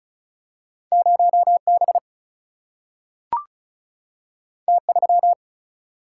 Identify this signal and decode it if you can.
Morse code, heard as a keyed tone, about 36 wpm, 700 Hz. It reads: 0B T3
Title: Single-Letter Single-Number 35wpm